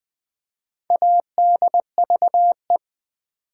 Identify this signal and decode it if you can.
Morse code, heard as a keyed tone, about 20 wpm, 700 Hz. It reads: ADVE
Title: Build A Word - 4 Letters At A Time - 11-Letter Words - 20wpm